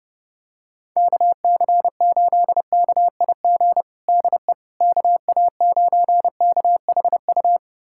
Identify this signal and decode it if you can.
Morse code, heard as a keyed tone, about 30 wpm, 700 Hz. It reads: KC8KIG DE KA9KHU
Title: USA Callsign DE Callsign - 6 Characters - 30wpm